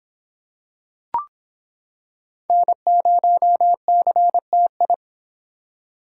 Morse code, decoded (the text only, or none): N0CTI